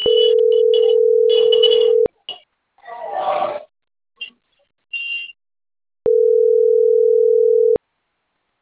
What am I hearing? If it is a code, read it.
Ringback tone (North America), 440 + 480 Hz.